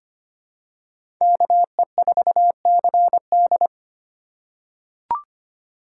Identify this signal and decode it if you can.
Morse code, heard as a keyed tone, about 25 wpm, 700 Hz. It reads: KE4CD
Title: USA Call Signs - 2x Character Spacing - 25wpm